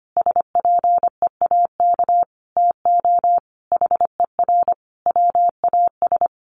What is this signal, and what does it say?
Morse code, heard as a keyed tone, about 25 wpm, 700 Hz. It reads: SPEAK TO HER WAS
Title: Sentences from Top 1500 Words 25wpm